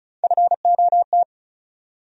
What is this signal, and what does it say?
Morse code, heard as a keyed tone, about 36 wpm, 700 Hz. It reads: FOT
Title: USA Call Sign Postfixes - 3 Letters 35wpm